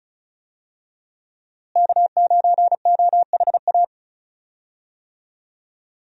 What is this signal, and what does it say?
Morse code, heard as a keyed tone, about 34 wpm, 700 Hz. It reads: K9OHA
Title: USA Call Signs - 5 Characters - Repeated 3x - 35wpm